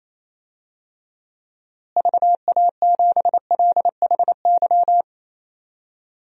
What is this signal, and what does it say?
Morse code, heard as a keyed tone, about 28 wpm, 700 Hz. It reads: VA7LHY
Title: DXCC-100 Call Signs - 6 Characters - Repeated 3x - 28wpm